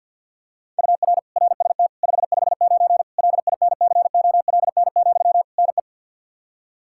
Morse code, heard as a keyed tone, about 47 wpm, 700 Hz. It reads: UR RST 559 LINCOLN, NE
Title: Practice QSO - Exchange 1 - Beginning - 50wpm